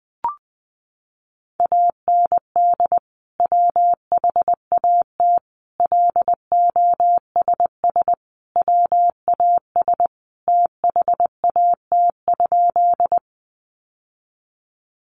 Morse code, heard as a keyed tone, about 20 wpm, 700 Hz. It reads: AND WHAT LOSS WAS THAT?
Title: Sentences from Top 1400 Words 20wpm